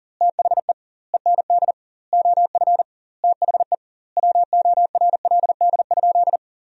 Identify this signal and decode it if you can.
Morse code, heard as a keyed tone, about 40 wpm, 700 Hz. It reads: THE END OF THE WORLD?
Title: 2x Word-Spacing - Sentences from Top 200 Words - 40wpm